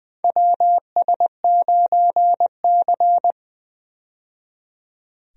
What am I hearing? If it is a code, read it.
Morse code, heard as a keyed tone, about 20 wpm, 700 Hz. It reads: WS9C
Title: USA Call Signs - 4 Characters - 20wpm